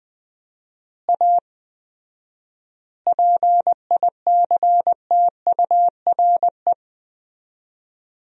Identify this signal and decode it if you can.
Morse code, heard as a keyed tone, about 21 wpm, 700 Hz. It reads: A PICTURE
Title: 4x Word-Spacing - Sentences from Top 500 Words - 20wpm